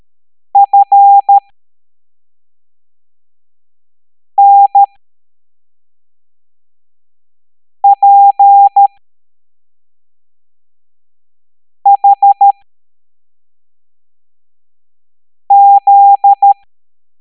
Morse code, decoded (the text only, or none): FNPHZ